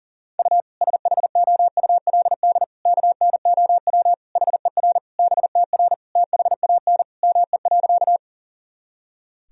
Morse code, heard as a keyed tone, about 37 wpm, 700 Hz. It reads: U SHOULD KNOW HER BTR THAN ME.
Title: Sentences from Top 400 Words and Abbreviations 40wpm